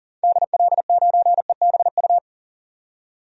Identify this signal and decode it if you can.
Morse code, heard as a keyed tone, about 37 wpm, 700 Hz. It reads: DL9EBU
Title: DXCC-100 Call Signs - 6 Characters - 40wpm